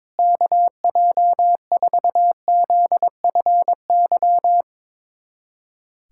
Morse code, heard as a keyed tone, about 22 wpm, 700 Hz. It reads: KJ4ZFY
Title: USA Callsign DE Callsign - 6 Characters - 22wpm